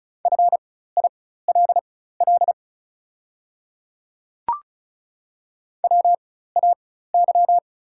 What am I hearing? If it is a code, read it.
Morse code, heard as a keyed tone, about 36 wpm, spaced as for 16 wpm, 700 Hz. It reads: FILL WAY
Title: 4x Character Spacing - Top 500 Words - 35wpm